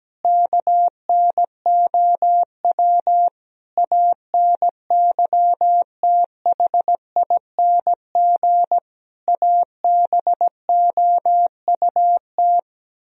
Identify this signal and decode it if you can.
Morse code, heard as a keyed tone, about 17 wpm, 700 Hz. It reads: KNOW ANYTHING ABOUT
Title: Speed-Racing - Sets of 5 Words 17wpm